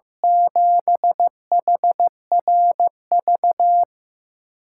Morse code, heard as a keyed tone, about 16 wpm, 700 Hz. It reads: 7HRV
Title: USA Call Signs - 8x Character Spacing - 15wpm